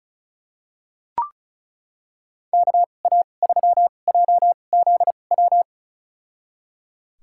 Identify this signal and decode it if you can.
Morse code, heard as a keyed tone, about 36 wpm, spaced as for 25 wpm, 700 Hz. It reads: KA3JZW
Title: USA Call Signs - 2x Character Spacing - 35wpm